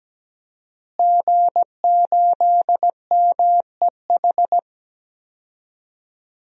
Morse code, heard as a keyed tone, about 17 wpm, 700 Hz. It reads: G8MEH
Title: DXCC-100 Call Signs - 5 Characters - Repeated 3x - 17wpm